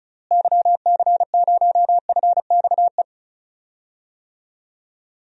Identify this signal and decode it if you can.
Morse code, heard as a keyed tone, about 36 wpm, 700 Hz. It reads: YC0FXE
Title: DXCC-100 Call Signs - 6 Characters - Repeated 2x - 35wpm